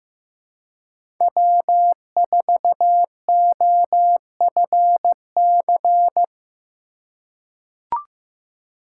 Morse code, heard as a keyed tone, about 15 wpm, 700 Hz. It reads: W4OFC
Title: USA Call Signs - 6x Character Spacing - 15wpm